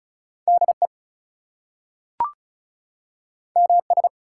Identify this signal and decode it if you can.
Morse code, heard as a keyed tone, about 36 wpm, 700 Hz. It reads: DE MS
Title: US States and Canadian Provinces 35wpm